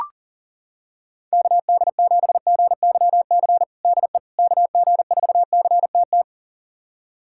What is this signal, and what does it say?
Morse code, heard as a keyed tone, about 40 wpm, 700 Hz. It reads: KD7GYC DE KG4CTT
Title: USA Callsign DE Callsign - 6 Characters - 40wpm